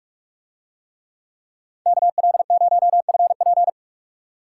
Morse code, heard as a keyed tone, about 45 wpm, 700 Hz. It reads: KL0FP